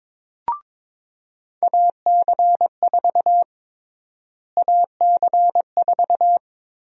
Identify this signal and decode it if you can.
Morse code, heard as a keyed tone, about 21 wpm, 700 Hz. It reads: AC4 AC4
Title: USA Call Sign Prefixes - Repeated 2x - 22wpm